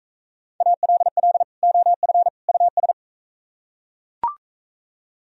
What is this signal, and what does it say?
Morse code, heard as a keyed tone, about 44 wpm, 700 Hz. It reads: ALL OF US